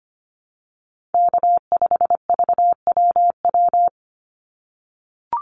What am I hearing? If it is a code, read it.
Morse code, heard as a keyed tone, about 25 wpm, 700 Hz. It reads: K5VWW
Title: Call Signs - 25wpm